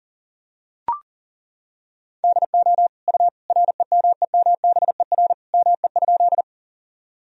Morse code, heard as a keyed tone, about 40 wpm, 700 Hz. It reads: DO U REMEMBER ME?
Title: Sentences from Top 400 Words and Abbreviations 40wpm